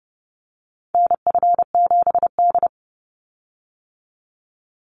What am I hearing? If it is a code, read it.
Morse code, heard as a keyed tone, about 30 wpm, 700 Hz. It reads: NF7D
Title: Call Signs - 30wpm